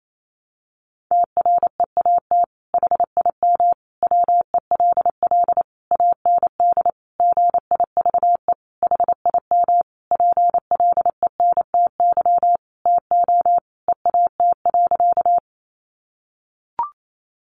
Morse code, heard as a keyed tone, about 28 wpm, 700 Hz. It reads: TREAT HIM WELL AND GIVE HIM PLENTY TO EAT.
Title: Sentences from Top 1300 Words 28wpm